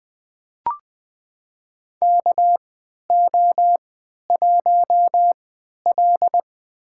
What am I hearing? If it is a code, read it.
Morse code, heard as a keyed tone, about 20 wpm, spaced as for 11 wpm, 700 Hz. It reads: KO1L
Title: USA Call Signs - 4 Characters - 3x Character Spacing - 20wpm